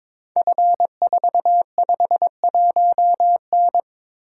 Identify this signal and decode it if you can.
Morse code, heard as a keyed tone, about 22 wpm, 700 Hz. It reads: F451N
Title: Code Groups - 5 Letter:Numbers - 22wpm